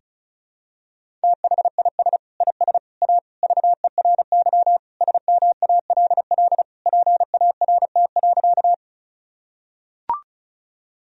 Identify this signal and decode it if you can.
Morse code, heard as a keyed tone, about 36 wpm, 700 Hz. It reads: THIS IS A VERY SMALL PART.
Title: Sentences from Top 1000 Words - Review Part 1 - 35wpm